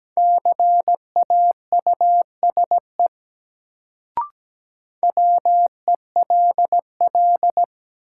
Morse code, heard as a keyed tone, about 17 wpm, 700 Hz. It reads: CAUSE WELL
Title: Sentences from Top 200 Words - New Word Drill - 17wpm